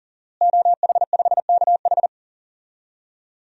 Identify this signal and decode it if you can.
Morse code, heard as a keyed tone, about 37 wpm, 700 Hz. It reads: OH5KH
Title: DXCC-100 Call Signs - 5 Characters - Repeated 2x - 40wpm